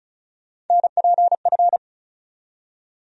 Morse code, heard as a keyed tone, about 34 wpm, 700 Hz. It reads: NPF
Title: USA Call Sign Postfixes - 3 Letters 35wpm